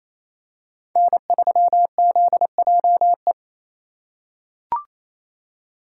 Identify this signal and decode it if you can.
Morse code, heard as a keyed tone, about 27 wpm, 700 Hz. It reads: N3ZJE
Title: USA Call Signs - 5 Characters - 8x Character Spacing - 28wpm